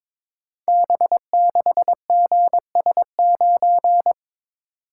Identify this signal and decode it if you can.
Morse code, heard as a keyed tone, about 22 wpm, 690 Hz. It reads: B6GS9
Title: Code Groups - 5 Letter:Numbers - 22wpm